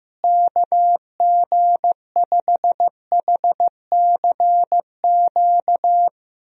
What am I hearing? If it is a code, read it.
Morse code, heard as a keyed tone, about 15 wpm, 700 Hz. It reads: KG5HCQ
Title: USA Call Signs - 6 Characters - Repeated 2x - 15wpm